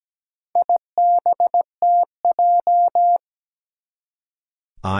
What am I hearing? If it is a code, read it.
Morse code, heard as a keyed tone, about 17 wpm, 700 Hz. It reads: IBTJ
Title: Four Challenging Letter-Number Combinations 17wpm